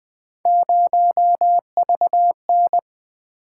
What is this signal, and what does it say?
Morse code, heard as a keyed tone, about 20 wpm, 700 Hz. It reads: 0VN
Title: Three Letter-Number Combinations - 20wpm